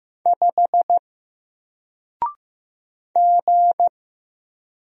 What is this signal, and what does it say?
Morse code, heard as a keyed tone, about 15 wpm, 700 Hz. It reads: 5 G
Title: Single Character (Letter:Number:Punctuation:Prosign) Alt 15wpm